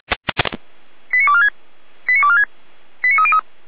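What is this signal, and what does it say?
Five-tone selective call (ZVEI), heard as quick runs of five tones, tones of about 80 ms. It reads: 89246 89246 89303
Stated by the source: ZVEI five-tone call received by SDR, logged by Feuerwehr Orlamünde fire brigade.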